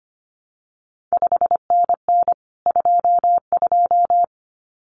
Morse code, heard as a keyed tone, about 25 wpm, 700 Hz. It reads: 5NN 22
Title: CQ WW DX Contest Exchange Encore - 25wpm